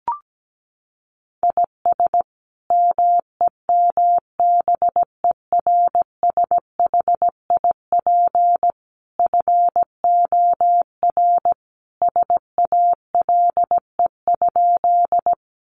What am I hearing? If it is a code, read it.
Morse code, heard as a keyed tone, about 17 wpm, 700 Hz. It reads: IS MEMBERSHIP FOR SALE?
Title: Sentences from Top 1400 Words 17wpm